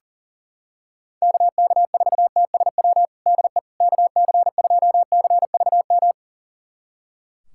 Morse code, heard as a keyed tone, about 37 wpm, 700 Hz. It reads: KK4TSW DE KC2CVM
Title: USA Callsign DE Callsign - 6 Characters - 40wpm